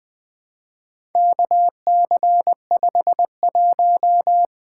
Morse code, heard as a keyed tone, about 20 wpm, 700 Hz. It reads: KC51
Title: Four Letter-Number Combinations - 20wpm